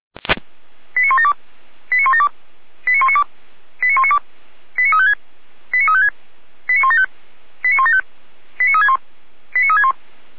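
Five-tone selective call (ZVEI), heard as quick runs of five tones, tones of about 70 ms. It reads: 89172 89172 89192 89192 89356 89356 89175 89175 89361 89361
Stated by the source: Received on an SDR.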